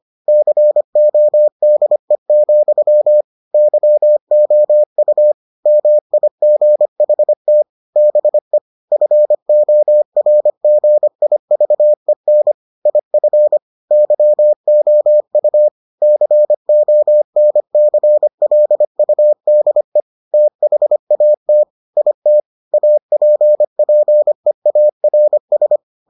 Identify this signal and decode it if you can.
Morse code, heard as a keyed tone, about 25 wpm, 600 Hz. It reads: CODE, YOU MIGHT BE FORGIVEN IF YOU CONCLUDE THAT IT APPEARS